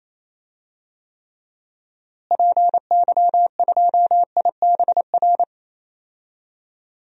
Morse code, heard as a keyed tone, about 28 wpm, 700 Hz. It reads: PY2IBR
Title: DXCC-100 Call Signs - 6 Characters - Repeated 2x - 28wpm